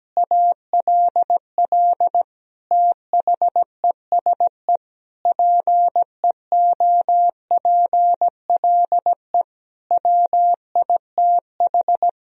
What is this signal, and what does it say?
Morse code, heard as a keyed tone, about 17 wpm, 700 Hz. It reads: ALL THESE PEOPLE WITH
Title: Sentences from Top 1000 Words - Review Part 1 - 17wpm